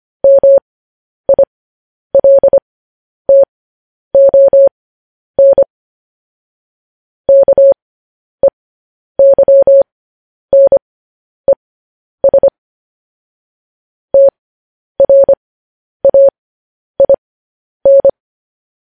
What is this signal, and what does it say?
Morse code, heard as a keyed tone, about 25 wpm, spaced as for 10 wpm, 550 Hz. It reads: MILTON KEYNES TRAIN
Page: news headlines in morse code at 10 wpm